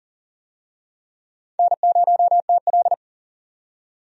Morse code, heard as a keyed tone, about 40 wpm, 700 Hz. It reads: N0TL
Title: USA Call Signs - 4 Characters - 40wpm